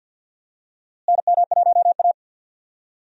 Morse code, heard as a keyed tone, about 50 wpm, 700 Hz. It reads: NM1A
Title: USA Call Signs - 4 Characters - Repeated 2x - 50wpm